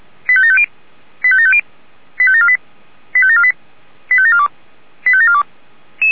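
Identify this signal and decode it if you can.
Five-tone selective call (ZVEI), heard as quick runs of five tones, tones of about 70 ms. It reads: 85750 85750 85748 85748 85742 85742
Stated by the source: ZVEI five-tone call received by SDR, recorded by feuerwehr-orlamuende fire brigade server.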